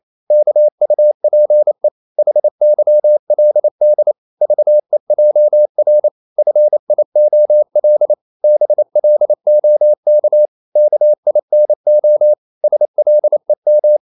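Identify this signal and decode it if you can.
Morse code, heard as a keyed tone, about 28 wpm, 600 Hz. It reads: KUPE HYLD VEJR FIOL BLOK KINO SLEM
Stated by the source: Danish short words sent in Morse at 28 wpm, from CW med Gnister.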